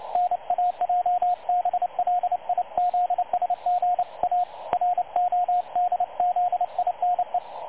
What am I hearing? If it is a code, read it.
Morse code, heard as a keyed tone, about 28 wpm, 720 Hz, with radio noise. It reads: NAJBLIZSGA RODZINE
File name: dlazaawansowanych / nagraniazpasma / websdr recording 3565.0kHz